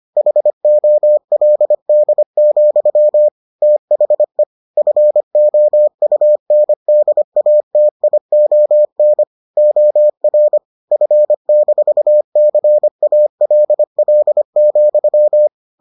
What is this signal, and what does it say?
Morse code, heard as a keyed tone, about 25 wpm, 600 Hz. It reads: HOLD, THE FOUNDATION OR F-CALL,